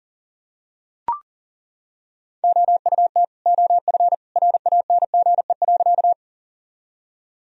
Morse code, heard as a keyed tone, about 37 wpm, 700 Hz. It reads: OUT OF RANGE.
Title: Sentences from Top 1000 Words - Review Part 4 - 40wpm